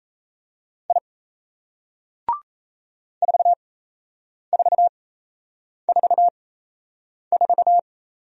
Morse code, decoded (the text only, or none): I 4444